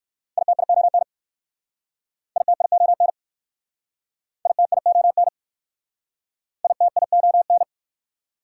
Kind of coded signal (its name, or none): Morse code